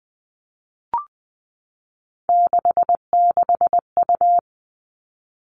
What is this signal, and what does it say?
Morse code, heard as a keyed tone, about 20 wpm, 700 Hz. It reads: 66U
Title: Three Challenging Letter-Number Combinations 20wpm